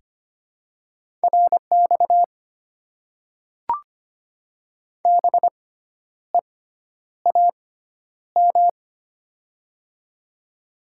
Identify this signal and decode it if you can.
Morse code, heard as a keyed tone, about 25 wpm, spaced as for 10 wpm, 700 Hz. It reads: RX BEAM
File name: Top 90 QSO Elements - 6x Character Spacing - 25wpm